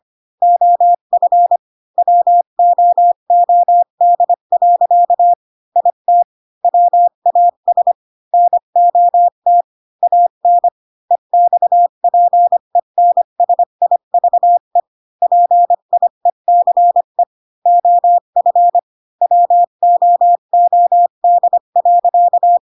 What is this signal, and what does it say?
Morse code, heard as a keyed tone, about 25 wpm, 700 Hz. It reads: OF WOOD. IT WAS NOT AN EXPENSIVE PIECE OF WOOD.
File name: Pinocchio / Pinocchio ch1 20 25 ik1hge